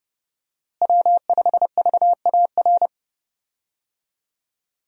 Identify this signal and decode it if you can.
Morse code, heard as a keyed tone, about 30 wpm, 700 Hz. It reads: W5VAR
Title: USA Call Signs - 5 Characters - Repeated 3x - 30wpm